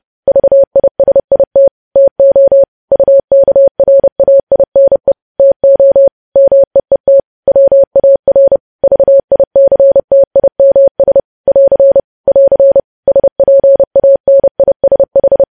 Morse code, heard as a keyed tone, about 30 wpm, 550 Hz. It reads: VISIT TO UKRAINE TO MEET WAR VICTIMS <AR> <AR> SPANISH
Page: news headlines in morse code at 30 wpm